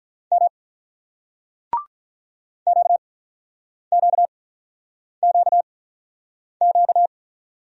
Morse code, heard as a keyed tone, about 44 wpm, 700 Hz. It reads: M Q Q Q Q